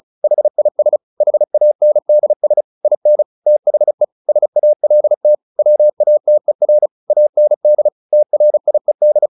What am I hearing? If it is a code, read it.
Morse code, heard as a keyed tone, about 34 wpm, 600 Hz. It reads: HIS HANDS IN THE SALT WATER AND TRIED
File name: the old man and the sea 2 35wpm